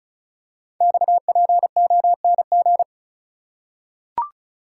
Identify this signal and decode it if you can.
Morse code, heard as a keyed tone, about 36 wpm, 690 Hz. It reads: XPONG
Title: Code Groups - 5 Letters - 35wpm